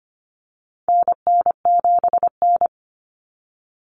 Morse code, heard as a keyed tone, about 25 wpm, 700 Hz. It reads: NN7N
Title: Common Cut Numbers - 4-Digit 25wpm